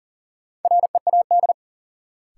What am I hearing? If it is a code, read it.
Morse code, heard as a keyed tone, about 40 wpm, 700 Hz. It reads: READ
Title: Sentences from Top 200 Words - New Word Drill - 40wpm